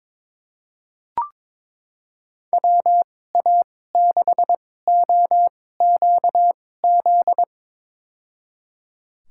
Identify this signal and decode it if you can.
Morse code, heard as a keyed tone, about 22 wpm, spaced as for 16 wpm, 700 Hz. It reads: WA6OQZ
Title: USA Call Signs - 2x Character Spacing - 22wpm